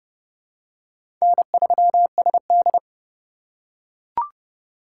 Morse code, heard as a keyed tone, about 30 wpm, 700 Hz. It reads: N3SD
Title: USA Call Signs - 4 Characters - 4x Character Spacing - 30wpm